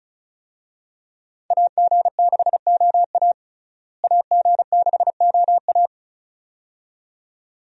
Morse code, heard as a keyed tone, about 34 wpm, 700 Hz. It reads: AG6OA AG6OA
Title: USA Call Signs - 5 Characters - Repeated 2x - 35wpm